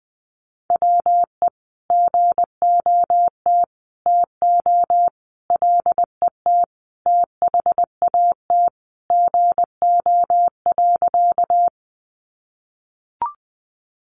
Morse code, heard as a keyed tone, about 20 wpm, 700 Hz. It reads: WE GOT TO LET THAT GO.
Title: Sentences from Top 300 Words 20wpm